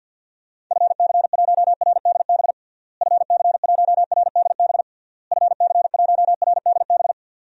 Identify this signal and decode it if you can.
Morse code, heard as a keyed tone, about 51 wpm, 700 Hz. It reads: FC1RDB FC1RDB FC1RDB
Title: DXCC-100 Call Signs - 6 Characters - Repeated 3x - 50wpm